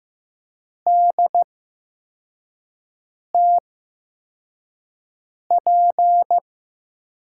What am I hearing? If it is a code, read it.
Morse code, heard as a keyed tone, about 15 wpm, spaced as for 4 wpm, 700 Hz. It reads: DTP
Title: USA Call Signs - 5 Characters - 8x Character Spacing - 15wpm